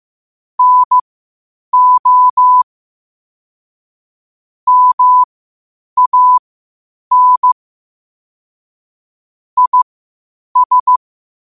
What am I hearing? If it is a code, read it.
Morse code, heard as a keyed tone, about 15 wpm, spaced as for 8 wpm, 1000 Hz. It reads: NO MAN IS
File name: qotdfiles / 8wpm